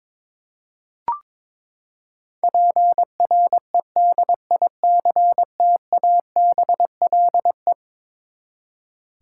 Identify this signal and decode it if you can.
Morse code, heard as a keyed tone, about 22 wpm, 700 Hz. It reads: PREDICTABLE
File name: practice / 22wpm / Eleven Letter Words - 22wpm